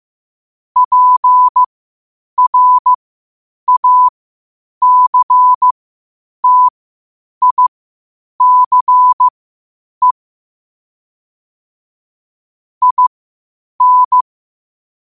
Morse code, heard as a keyed tone, about 15 wpm, spaced as for 8 wpm, 1000 Hz. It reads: PRACTICE IN
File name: qotdfiles / 8wpm